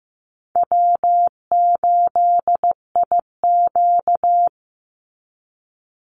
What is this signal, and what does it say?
Morse code, heard as a keyed tone, about 15 wpm, 700 Hz. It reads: W8IQ
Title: Easy Call Signs Repeated 3x - 15wpm